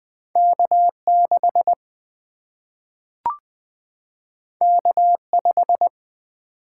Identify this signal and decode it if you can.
Morse code, heard as a keyed tone, about 20 wpm, 700 Hz. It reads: K6 K5
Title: Single-Letter Single-Number 20wpm